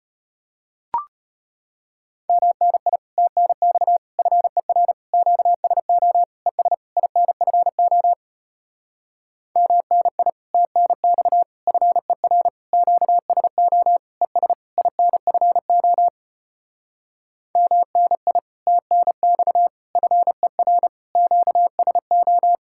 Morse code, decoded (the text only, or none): MNI TNX FER QSO ES INFO MNI TNX FER QSO ES INFO MNI TNX FER QSO